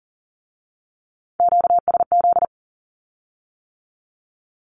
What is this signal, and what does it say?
Morse code, heard as a keyed tone, about 42 wpm, 700 Hz. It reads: QSZ
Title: Q-Codes with Spoken Meaning - 40wpm